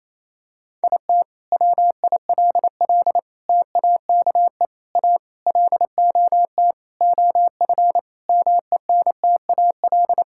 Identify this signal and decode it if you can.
Morse code, heard as a keyed tone, about 28 wpm, 700 Hz. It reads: IT WILL TAKE A LOT OF MENTAL
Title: Sentences from Top 1000 Words - Review Part 6 - 28wpm